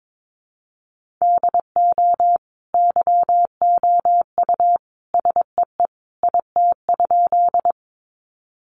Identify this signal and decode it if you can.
Morse code, heard as a keyed tone, about 22 wpm, 700 Hz. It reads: DO YOU SEE IT?
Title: Sentences from Top 100 Words 22wpm